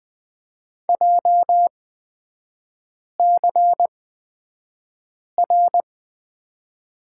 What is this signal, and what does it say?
Morse code, heard as a keyed tone, about 20 wpm, spaced as for 5 wpm, 690 Hz. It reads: JCR
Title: 236 Lesson 38e - Letter q - Three Characters 20wpm